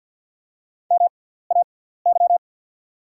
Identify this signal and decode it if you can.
Morse code, heard as a keyed tone, about 48 wpm, spaced as for 17 wpm, 690 Hz. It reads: MAY
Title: 6x Character Spacing - Top 200 Words - 50wpm